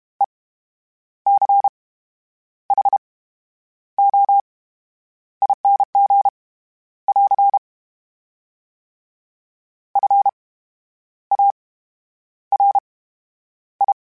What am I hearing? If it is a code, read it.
Morse code, heard as a keyed tone, about 32 wpm, spaced as for 9 wpm, 800 Hz. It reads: ECHOING <AR> FARI
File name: Special INGs / Special INGs-mp3s / VerySlowToFaster Ing @30